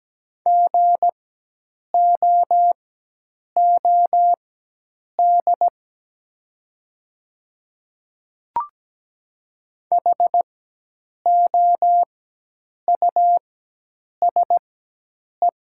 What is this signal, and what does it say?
Morse code, heard as a keyed tone, about 17 wpm, spaced as for 8 wpm, 700 Hz. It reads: GOOD HOUSE
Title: 4x Character Spacing - Top 300 Words - 17wpm